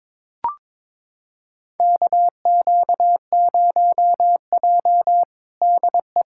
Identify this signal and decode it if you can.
Morse code, heard as a keyed tone, about 22 wpm, 700 Hz. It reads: KQ0J DE
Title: USA Callsign DE Callsign - 4 Characters - 22wpm